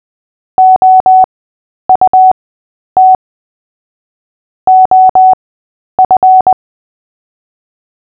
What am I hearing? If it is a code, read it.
Morse code, heard as a keyed tone, about 20 wpm, spaced as for 10 wpm, 750 Hz. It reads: OUT OF
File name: cw / fists / Short-Story-8-10wpm